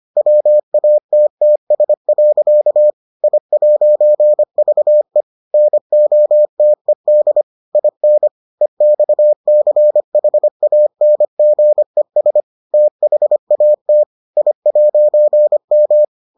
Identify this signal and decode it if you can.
Morse code, heard as a keyed tone, about 25 wpm, 600 Hz. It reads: WATTS. I'VE NOTED IN EXCHANGES THAT I'M